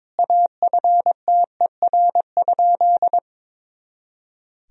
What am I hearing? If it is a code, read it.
Morse code, heard as a keyed tone, about 23 wpm, 700 Hz. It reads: AFTER?
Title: Sentences from Top 200 Words and Abbreviations 22wpm